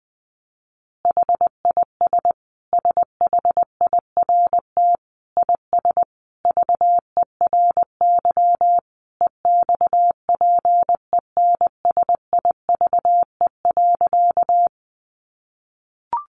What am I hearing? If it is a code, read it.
Morse code, heard as a keyed tone, about 20 wpm, 700 Hz. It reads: HIS SHIRT IS VERY EXPENSIVE.
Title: Additional Easy Phrases 20wpm